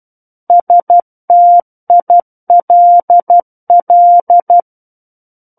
Morse code, heard as a keyed tone, about 12 wpm, 700 Hz. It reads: STILL
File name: Book One / 03-Chapter 3 - 12 WPM